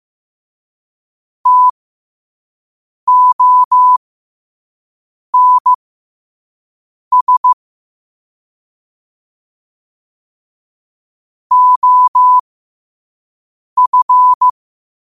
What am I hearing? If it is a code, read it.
Morse code, heard as a keyed tone, about 15 wpm, spaced as for 5 wpm, 1000 Hz. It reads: TONS OF
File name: qotdfiles / 5wpm